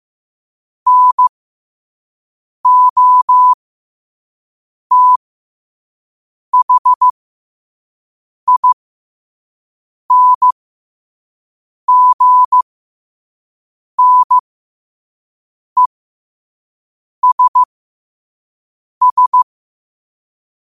Morse code, heard as a keyed tone, about 15 wpm, spaced as for 5 wpm, 1000 Hz. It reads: NOTHINGNESS
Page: New quotes every day in morse code at 5 Words per minute.